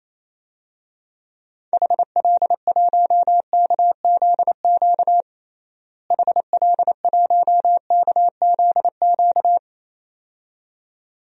Morse code, heard as a keyed tone, about 28 wpm, 700 Hz. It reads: HL1KZQ HL1KZQ
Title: DXCC-100 Call Signs - 6 Characters - Repeated 2x - 28wpm